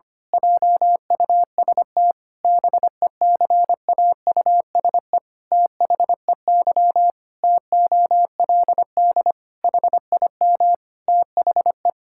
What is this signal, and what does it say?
Morse code, heard as a keyed tone, about 25 wpm, 700 Hz. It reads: JUST BECAUSE THEY TOLD HIM THE
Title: Pangram Challenge v2 - 25wpm